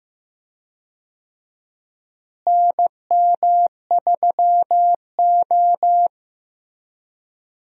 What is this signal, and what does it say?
Morse code, heard as a keyed tone, about 15 wpm, 700 Hz. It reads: NM3O
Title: USA Call Signs - 4 Characters - Repeated 3x - 15wpm